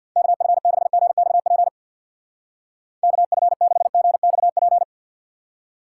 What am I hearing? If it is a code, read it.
Morse code, heard as a keyed tone, about 60 wpm, 700 Hz. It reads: KF6GXP KF6GXP
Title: Speed Racing - USA Call Signs 50wpm